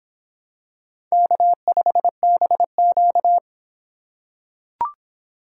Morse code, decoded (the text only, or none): K5BQ